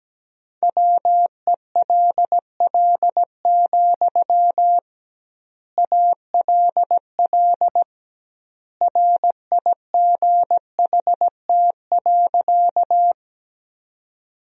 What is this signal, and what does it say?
Morse code, heard as a keyed tone, about 17 wpm, 700 Hz. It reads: WELL, ALL RIGHT.
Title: 2x Word-Spacing - Sentences from Top 200 Words - 17wpm